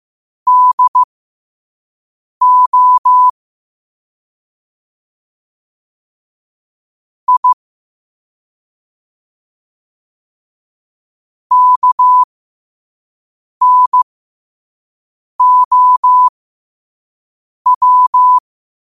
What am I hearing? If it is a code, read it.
Morse code, heard as a keyed tone, about 15 wpm, spaced as for 5 wpm, 1000 Hz. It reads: DO I KNOW
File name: qotdfiles / 5wpm